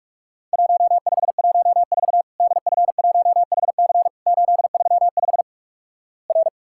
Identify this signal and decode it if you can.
Morse code, heard as a keyed tone, about 46 wpm, 700 Hz. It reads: 1514 DF1HC 835 R
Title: Worked All Europe DX Contest - QTC Exchange - 45wpm